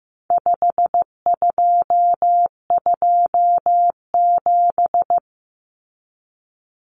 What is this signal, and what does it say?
Morse code, heard as a keyed tone, about 15 wpm, 700 Hz. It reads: 5227
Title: Common Cut Numbers - 4-Digit 15wpm